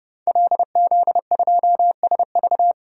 Morse code, heard as a keyed tone, about 31 wpm, 700 Hz. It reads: LZ2SV
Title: DXCC-100 Call Signs - 5 Characters - 30wpm